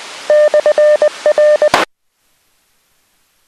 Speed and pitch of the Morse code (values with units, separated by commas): 20 wpm, 600 Hz